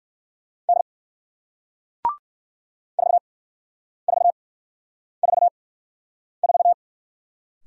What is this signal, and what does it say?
Morse code, heard as a keyed tone, about 56 wpm, spaced as for 9 wpm, 700 Hz. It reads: D 4444
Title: Speed-Racing Single Letter-Number 45wpm